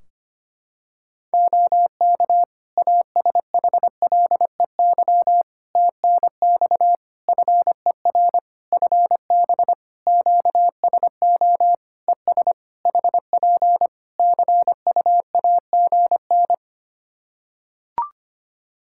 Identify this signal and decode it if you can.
Morse code, heard as a keyed tone, about 25 wpm, 700 Hz. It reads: OK ASHLEY TNX FER FB QSO ES HP CUAGN
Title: Practice QSO - Exchange 4 - Ending - 25wpm